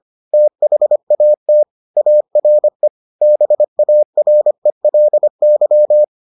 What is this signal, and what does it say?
Morse code, heard as a keyed tone, about 25 wpm, 600 Hz. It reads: THAT ARE BARELY